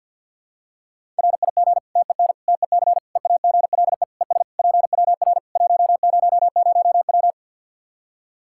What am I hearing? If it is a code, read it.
Morse code, heard as a keyed tone, about 52 wpm, 700 Hz. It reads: RIG TEN TEC EAGLE ES PWR 100W
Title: Practice QSO - Exchange 2 - Info - 50wpm